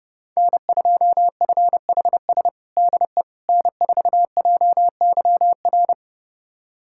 Morse code, heard as a keyed tone, about 32 wpm, 700 Hz. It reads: N2FHS DE N4JYR
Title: USA Callsign DE Callsign - 5 Characters - 30wpm